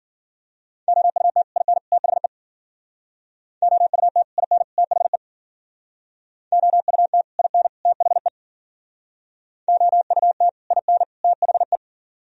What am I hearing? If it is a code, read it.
Morse code, heard as a keyed tone, about 52 wpm, 700 Hz. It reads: OUT IN THE OUT IN THE OUT IN THE OUT IN THE